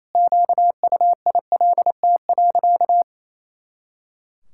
Morse code, heard as a keyed tone, about 27 wpm, 700 Hz. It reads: QUILT.
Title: Pangram Challenge v2 - 28wpm